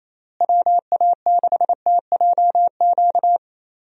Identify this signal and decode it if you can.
Morse code, heard as a keyed tone, about 28 wpm, 700 Hz. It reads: WA6TJQ
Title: USA Call Signs - 6 Characters - Repeated 3x - 28wpm